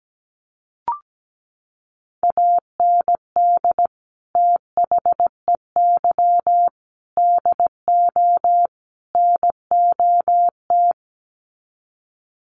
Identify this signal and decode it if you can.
Morse code, heard as a keyed tone, about 17 wpm, 700 Hz. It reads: AND THEY DO NOT
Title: Sets of 4 Words 17wpm